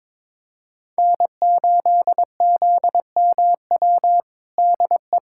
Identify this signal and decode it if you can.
Morse code, heard as a keyed tone, about 22 wpm, 700 Hz. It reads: N8ZMW DE
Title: USA Callsign DE Callsign - 5 Characters - 22wpm